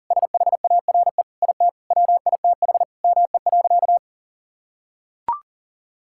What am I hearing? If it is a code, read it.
Morse code, heard as a keyed tone, about 40 wpm, 700 Hz. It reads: SHARE IT WITH ME.
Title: Sentences from Top 1000 Words - Review Part 6 - 40wpm